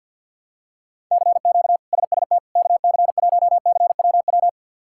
Morse code, heard as a keyed tone, about 49 wpm, 700 Hz. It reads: CQ SST KC1CWW
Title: K1USN SST QSO Practice - 50wpm